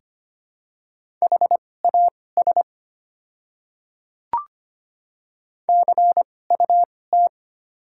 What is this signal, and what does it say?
Morse code, heard as a keyed tone, about 25 wpm, spaced as for 18 wpm, 690 Hz. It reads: HAS CUT
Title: 2x Character Spacing - Top 300 Words - 25wpm